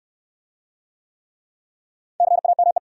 Morse code, heard as a keyed tone, about 68 wpm, 700 Hz. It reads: CAME